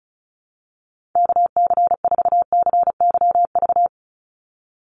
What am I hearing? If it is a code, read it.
Morse code, heard as a keyed tone, about 34 wpm, 700 Hz. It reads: KC4CYV
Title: USA Call Signs 35wpm